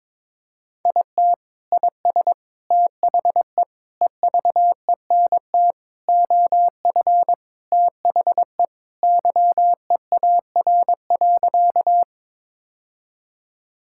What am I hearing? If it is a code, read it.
Morse code, heard as a keyed tone, about 22 wpm, 700 Hz. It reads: IT IS THE EVENT OF THE YEAR.
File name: practice / 22wpm / Sentences with 3-Word Phrases - 22wpm